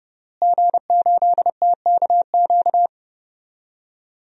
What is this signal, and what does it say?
Morse code, heard as a keyed tone, about 30 wpm, 700 Hz. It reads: G8TKQ
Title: DXCC-100 Call Signs - 5 Characters - Repeated 2x - 30wpm